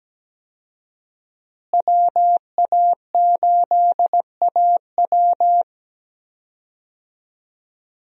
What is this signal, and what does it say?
Morse code, heard as a keyed tone, about 17 wpm, 700 Hz. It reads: WA8AW
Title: USA Call Signs - 5 Characters - Repeated 2x - 17wpm